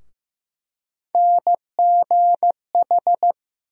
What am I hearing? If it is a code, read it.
Morse code, heard as a keyed tone, about 15 wpm, 700 Hz. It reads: NGH
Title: USA Call Sign Postfixes - 3 Letters 15wpm